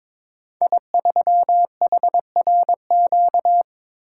Morse code, heard as a keyed tone, about 23 wpm, 700 Hz. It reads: I3HRQ
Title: Code Groups - 5 Letter:Numbers - 22wpm